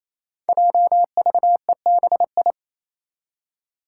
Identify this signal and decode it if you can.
Morse code, heard as a keyed tone, about 27 wpm, 700 Hz. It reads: JVEBI
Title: Five Challenging Letter-Number Combinations 28wpm